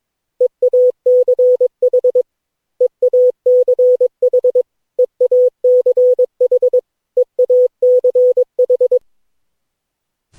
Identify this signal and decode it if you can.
Morse code, heard as a keyed tone, about 22 wpm, 500 Hz. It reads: EACH EACH EACH EACH
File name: cw / Session9 cw training